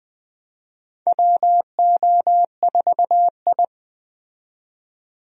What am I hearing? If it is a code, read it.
Morse code, heard as a keyed tone, about 20 wpm, 700 Hz. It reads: WO4I
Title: USA Call Signs - 4 Characters - 20wpm